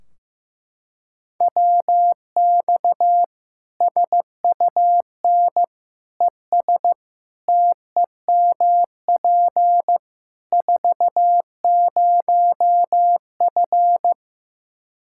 Morse code, heard as a keyed tone, about 15 wpm, 700 Hz. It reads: WX SUN ES TEMP 40F
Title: Practice QSO - Exchange 2 - Info - 15wpm